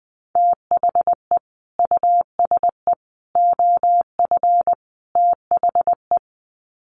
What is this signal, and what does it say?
Morse code, heard as a keyed tone, about 20 wpm, 700 Hz. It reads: THE USE OF THE
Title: Sets of 4 Words 20wpm